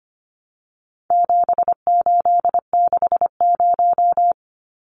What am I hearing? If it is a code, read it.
Morse code, heard as a keyed tone, about 25 wpm, 700 Hz. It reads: 7860
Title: Numbers - 4-Digit 25wpm